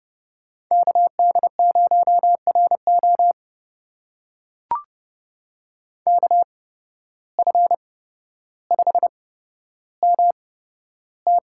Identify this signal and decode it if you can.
Morse code, heard as a keyed tone, about 30 wpm, spaced as for 13 wpm, 700 Hz. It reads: KD0RO KF5MT
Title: USA Call Signs - 5 Characters - 8x Character Spacing - 30wpm